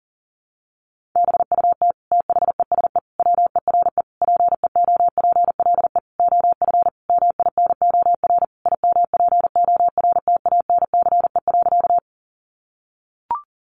Morse code, heard as a keyed tone, about 40 wpm, 700 Hz. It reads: BUT THESE WERE PEOPLE OF MINOR IMPORTANCE.
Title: Sentences from Top 1500 Words 40wpm